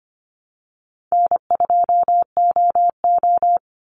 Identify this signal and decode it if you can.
Morse code, heard as a keyed tone, about 25 wpm, 700 Hz. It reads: N2OO
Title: Call Signs - 25wpm